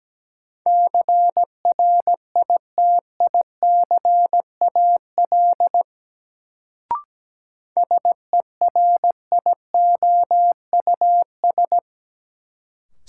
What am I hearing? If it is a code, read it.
Morse code, heard as a keyed tone, about 17 wpm, 700 Hz. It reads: CRITICAL SERIOUS
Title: Sentences from Top 1100 Words - New Word Drill Alt - 17wpm